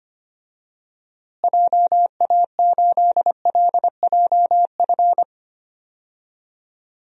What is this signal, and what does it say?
Morse code, heard as a keyed tone, about 25 wpm, 700 Hz. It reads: JA8LJF
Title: DXCC-100 Call Signs - 6 Characters - Repeated 3x - 25wpm